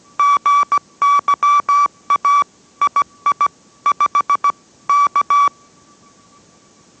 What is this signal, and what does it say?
Morse code, heard as a keyed tone, about 17 wpm, 1150 Hz. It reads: GYA II 5 K